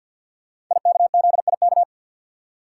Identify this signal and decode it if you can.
Morse code, heard as a keyed tone, about 46 wpm, 700 Hz. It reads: IKZIK
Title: Code Groups - 5 Letters - 50wpm